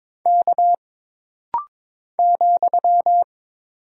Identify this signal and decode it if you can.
Morse code, heard as a keyed tone, about 22 wpm, 700 Hz. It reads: K ,
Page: This rapid-fire timing is geared to push your ICR (Instant Character Recognition) skills.